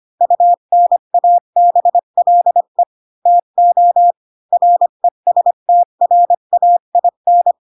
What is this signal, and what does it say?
Morse code, heard as a keyed tone, about 25 wpm, 700 Hz. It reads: UNABLE TO RESTRAIN
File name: Pinocchio / Pinocchio ch8 20 25 ik1hge